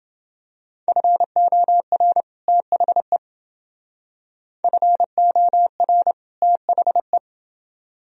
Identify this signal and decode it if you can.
Morse code, heard as a keyed tone, about 28 wpm, 700 Hz. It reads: FOR THE FOR THE